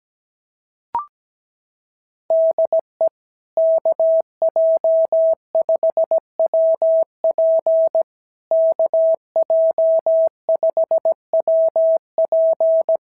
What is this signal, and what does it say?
Morse code, heard as a keyed tone, about 16 wpm, 650 Hz. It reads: DE KJ5WP KJ5WP
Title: SOTA QSO Practice 17wpm